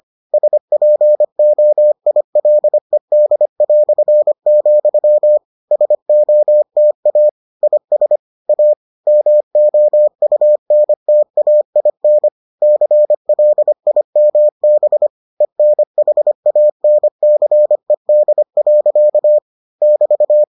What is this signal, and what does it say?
Morse code, heard as a keyed tone, about 25 wpm, 600 Hz. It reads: SPOILED", SOTA IS A MOUNTAIN CLIMB ENHANCED. <BT>